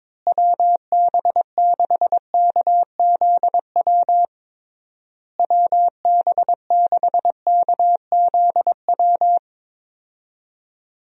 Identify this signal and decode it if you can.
Morse code, heard as a keyed tone, about 22 wpm, 700 Hz. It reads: WB6KZW WB6KZW
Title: USA Call Signs - 6 Characters - Repeated 3x - 22wpm